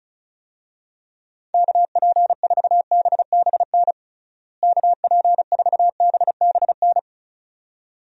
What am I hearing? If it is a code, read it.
Morse code, heard as a keyed tone, about 36 wpm, 700 Hz. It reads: KP4BBN KP4BBN
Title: DXCC-100 Call Signs - 6 Characters - Repeated 2x - 35wpm